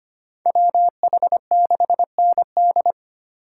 Morse code, heard as a keyed tone, about 30 wpm, 700 Hz. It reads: WH6ND
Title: Speed Racing - USA Call Signs 25wpm